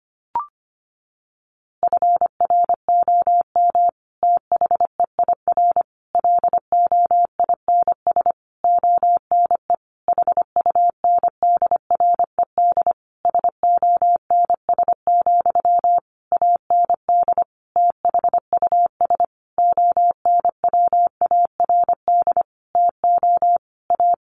Morse code, heard as a keyed tone, about 25 wpm, 700 Hz. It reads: FROM THEIR LOINS ONE HUNDRED SONS, AND THUS ONWARD TO A
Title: A Sound Of Thunder 25wpm